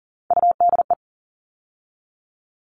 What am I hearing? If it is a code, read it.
Morse code, heard as a keyed tone, about 40 wpm, 700 Hz. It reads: UDE
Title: All Cut Numbers - 3-Digit 40wpm